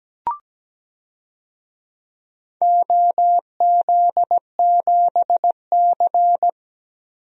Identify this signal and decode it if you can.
Morse code, heard as a keyed tone, about 17 wpm, 700 Hz. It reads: OZ7C
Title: DXCC-100 Call Signs - 4 Characters - 17wpm